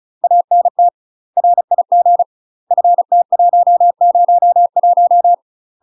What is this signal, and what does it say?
Morse code, heard as a keyed tone, about 34 wpm, 700 Hz. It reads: ANT RIG FT101
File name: UK MASTER QSOs TRAINING / 35WPM / UK MASTER QSOs 0008